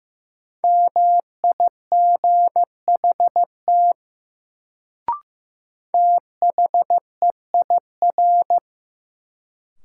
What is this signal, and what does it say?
Morse code, heard as a keyed tone, about 15 wpm, 700 Hz. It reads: MIGHT THEIR
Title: Sets of 2 Words - New Word Drill - 15wpm